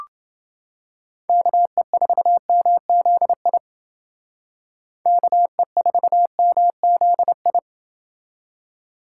Morse code, heard as a keyed tone, about 28 wpm, 700 Hz. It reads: KE4MZI KE4MZI